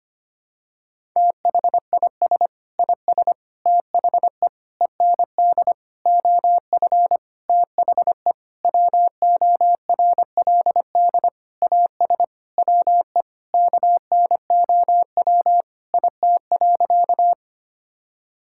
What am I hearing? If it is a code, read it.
Morse code, heard as a keyed tone, about 25 wpm, 700 Hz. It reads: THIS IS THE END OF THE WORLD AS WE KNOW IT.
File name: practice / 25wpm / Sentences with 9-Word Phrases - 25wpm